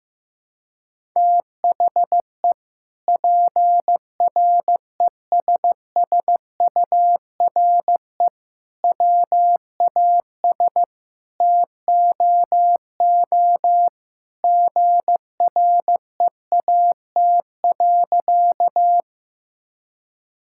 Morse code, heard as a keyed tone, about 15 wpm, 700 Hz. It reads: THE PRESSURE WAS TOO GREAT.
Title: Sentences from Top 1000 Words - Review Part 6 - 15wpm